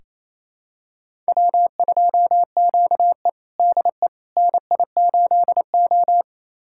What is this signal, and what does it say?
Morse code, heard as a keyed tone, about 28 wpm, 700 Hz. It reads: W2QE DE NI8O
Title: USA Callsign DE Callsign - 4 Characters - 28wpm